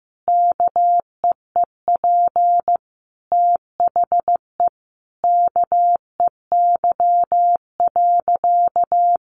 Morse code, heard as a keyed tone, about 15 wpm, 700 Hz. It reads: KEEP THE KEY.
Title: Sentences from Top 1300 Words 15wpm